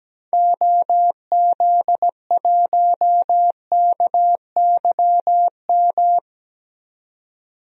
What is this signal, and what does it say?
Morse code, heard as a keyed tone, about 17 wpm, 700 Hz. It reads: OZ1KYM
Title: DXCC-100 Call Signs - 6 Characters - 17wpm